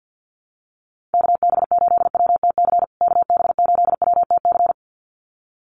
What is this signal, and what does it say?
Morse code, heard as a keyed tone, about 50 wpm, 700 Hz. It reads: KB8WTC KB8WTC
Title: USA Call Signs Repeated 2x 50wpm